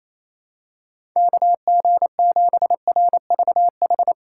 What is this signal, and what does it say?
Morse code, heard as a keyed tone, about 28 wpm, 700 Hz. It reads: KG7RVH
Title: USA Call Signs - 6 Characters - 3x Character Spacing - 28wpm